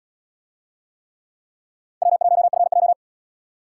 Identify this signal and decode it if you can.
Morse code, heard as a keyed tone, about 71 wpm, 700 Hz. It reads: W1FJ